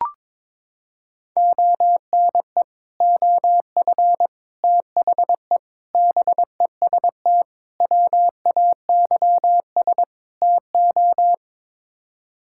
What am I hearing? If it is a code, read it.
Morse code, heard as a keyed tone, about 22 wpm, 700 Hz. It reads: ONE OF THE BEST WAYS TO
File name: practice / 22wpm / Sets of 6 Words - 22wpm